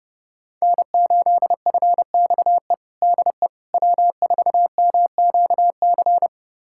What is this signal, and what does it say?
Morse code, heard as a keyed tone, about 30 wpm, 700 Hz. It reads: N8FXE DE W4MQC
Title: USA Callsign DE Callsign 30wpm